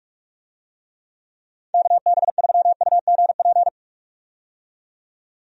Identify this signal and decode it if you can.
Morse code, heard as a keyed tone, about 46 wpm, 700 Hz. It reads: KB3UGP
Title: USA Call Signs - 6 Characters - 45wpm